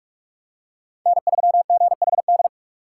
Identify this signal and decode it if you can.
Morse code, heard as a keyed tone, about 43 wpm, 700 Hz. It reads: N3GHD
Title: USA Call Signs - 5 Characters - 2x Character Spacing - 45wpm